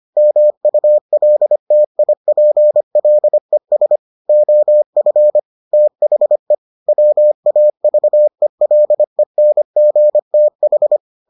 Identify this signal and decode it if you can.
Morse code, heard as a keyed tone, about 25 wpm, 600 Hz. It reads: MULTIPLES OF THE WAVELENGTH